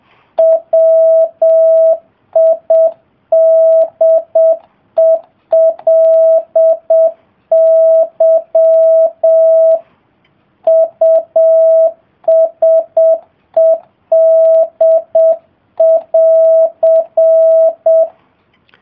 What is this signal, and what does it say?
Morse code, heard as a keyed tone, about 7 wpm, 650 Hz. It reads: WIDELY USED<AR>